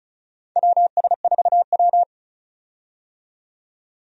Morse code, heard as a keyed tone, about 34 wpm, 700 Hz. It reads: WS4W
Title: USA Call Signs - 4 Characters - 35wpm